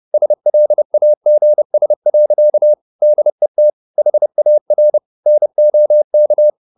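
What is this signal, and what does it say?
Morse code, heard as a keyed tone, about 30 wpm, 600 Hz. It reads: SLAGS. DET HAR NOK
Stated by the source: High-speed Morse (QRQ) at 30 wpm.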